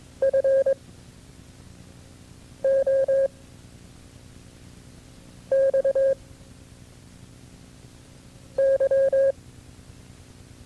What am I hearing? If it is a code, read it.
Morse code, heard as a keyed tone, about 23 wpm, 540 Hz. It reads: F O X Y
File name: FreeFiles / CW-practice / KB6MT-tape2-side1